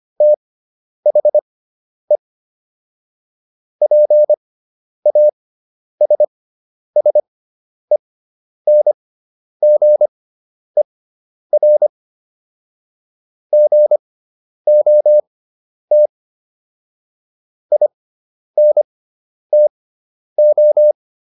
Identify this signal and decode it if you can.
Morse code, heard as a keyed tone, about 25 wpm, spaced as for 10 wpm, 600 Hz. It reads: THE PASSENGER GOT INTO